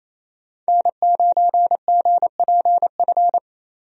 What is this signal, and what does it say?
Morse code, heard as a keyed tone, about 28 wpm, 700 Hz. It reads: N9GPF
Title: USA Call Signs - 5 Characters - 28wpm